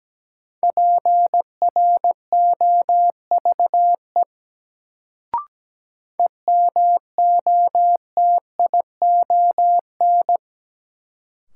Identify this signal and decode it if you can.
Morse code, heard as a keyed tone, about 17 wpm, 700 Hz. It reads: PROVE EMOTION
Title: Sentences from Top 1600 Words - New Word Drill - 17wpm